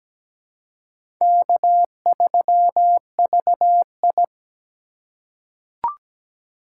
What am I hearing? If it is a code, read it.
Morse code, heard as a keyed tone, about 17 wpm, 700 Hz. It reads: K3VI
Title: USA Call Signs - 4 Characters - 4x Character Spacing - 17wpm